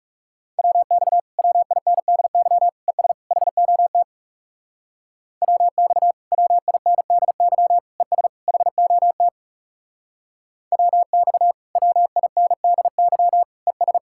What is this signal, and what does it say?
Morse code, heard as a keyed tone, about 38 wpm, 700 Hz. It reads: WX WINDY ES HOT WX WINDY ES HOT WX WINDY ES